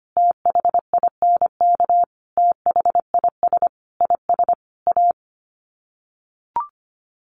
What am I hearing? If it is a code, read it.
Morse code, heard as a keyed tone, about 25 wpm, 700 Hz. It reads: THINK THIS IS A
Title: Sets of 5 Words 25wpm